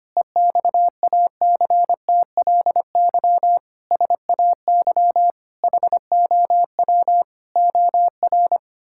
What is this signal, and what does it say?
Morse code, heard as a keyed tone, about 25 wpm, 700 Hz. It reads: EXACTLY SAY HOW OR
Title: Sentences from Top 1000 Words - Review Part 6 - 25wpm